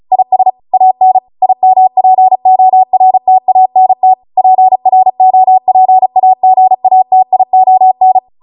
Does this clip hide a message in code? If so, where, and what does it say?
Yes: Morse code from 0.1 to 8.2 s, heard as a keyed tone, about 36 wpm, 750 Hz. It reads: IS AN IMPORTANT PROPAGATION